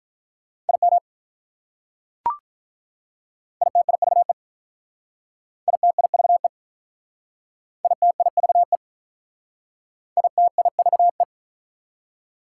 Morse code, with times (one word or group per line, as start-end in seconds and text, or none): 0.7-1.0 IM
3.6-4.3 ITIVE
5.7-6.5 ITIVE
7.8-8.8 ITIVE
10.2-11.2 ITIVE